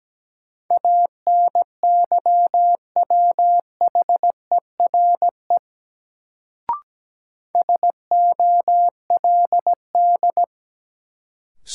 Morse code, heard as a keyed tone, about 17 wpm, 700 Hz. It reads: ANYWHERE SOLD
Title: Sentences from Top 1400 Words - New Word Drill - 17wpm